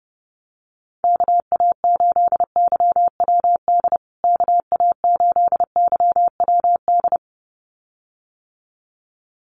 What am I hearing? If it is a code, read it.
Morse code, heard as a keyed tone, about 30 wpm, 700 Hz. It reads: KA8YWD KA8YWD
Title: USA Call Signs Repeated 2x 30wpm